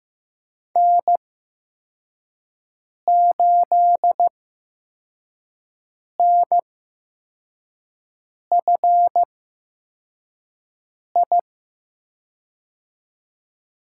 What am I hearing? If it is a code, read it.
Morse code, heard as a keyed tone, about 15 wpm, spaced as for 4 wpm, 700 Hz. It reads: N8NFI
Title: USA Call Signs - 8x Character Spacing - 15wpm